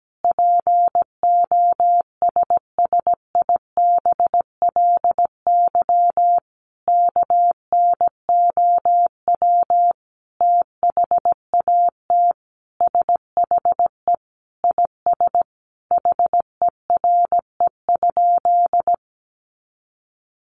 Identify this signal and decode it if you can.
Morse code, heard as a keyed tone, about 17 wpm, 700 Hz. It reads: POSSIBLY KNOW THAT SHE IS HERE?
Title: Sentences from Top 1300 Words 17wpm